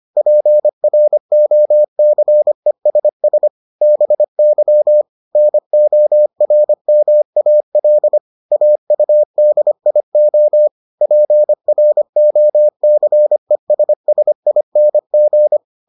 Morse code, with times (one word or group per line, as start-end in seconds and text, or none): 0.2-3.5 PROCESS
3.8-5.0 BY
5.4-8.2 NORMAL
8.5-10.7 AUDIO
11.0-15.6 PROCESSING